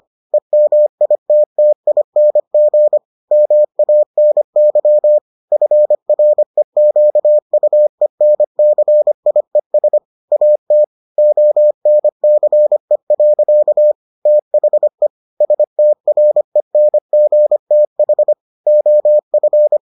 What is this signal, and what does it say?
Morse code, heard as a keyed tone, about 25 wpm, 600 Hz. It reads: EMITTING MANY FREQUENCIES AT ONCE. THE STRENGTH OF